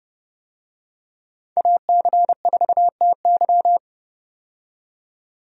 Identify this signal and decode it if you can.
Morse code, heard as a keyed tone, about 31 wpm, 700 Hz. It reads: AC4TY
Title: USA Call Signs - 5 Characters - 30wpm